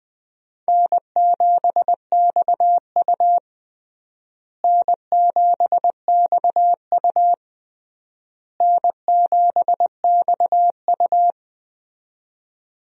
Morse code, heard as a keyed tone, about 20 wpm, 700 Hz. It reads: N7XU N7XU N7XU
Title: USA Call Signs - 4 Characters - Repeated 3x - 20wpm